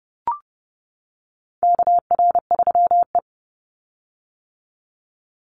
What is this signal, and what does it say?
Morse code, heard as a keyed tone, about 32 wpm, 700 Hz. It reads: KR3E
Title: Call Signs - 30wpm